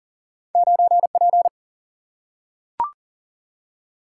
Morse code, heard as a keyed tone, about 39 wpm, 700 Hz. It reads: 9 P
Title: Single-Letter Single-Number 40wpm